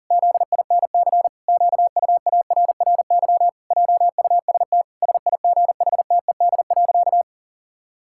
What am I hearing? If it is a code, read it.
Morse code, heard as a keyed tone, about 37 wpm, 700 Hz. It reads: ZINC QUARRY JUST SIGHTED.
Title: Pangram Challenge v2 - 40wpm